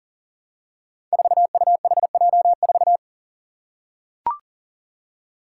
Morse code, heard as a keyed tone, about 39 wpm, 700 Hz. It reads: 4UHJ4
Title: Five Challenging Letter-Number Combinations 40wpm